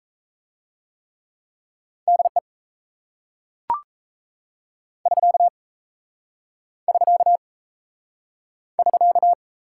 Morse code, heard as a keyed tone, about 36 wpm, 700 Hz. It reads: DE <SK> <SK> <SK>